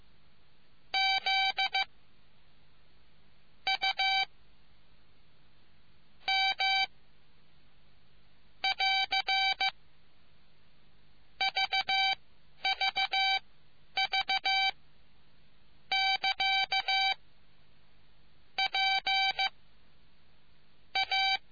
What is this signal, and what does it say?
Morse code, heard as a keyed tone, about 15 wpm, spaced as for 10 wpm, 770 Hz. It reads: Z U M <AR> VVV <KA> P A